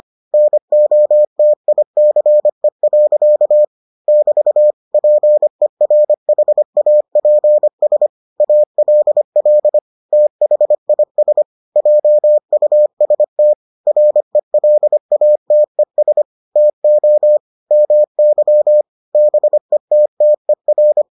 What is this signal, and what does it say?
Morse code, heard as a keyed tone, about 25 wpm, 600 Hz. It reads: NOTICE. <BT> PERHAPS ALL THIS JUST RELATES TO MY BETTER